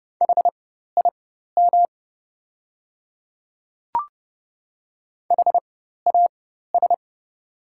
Morse code, heard as a keyed tone, about 30 wpm, spaced as for 14 wpm, 700 Hz. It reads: HIM HAS
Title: 4x Character Spacing - Top 100 Words - 30wpm